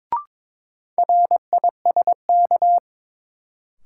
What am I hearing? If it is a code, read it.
Morse code, heard as a keyed tone, about 22 wpm, 700 Hz. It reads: RISK
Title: Sentences from Top 600 Words - New Word Drill - 22wpm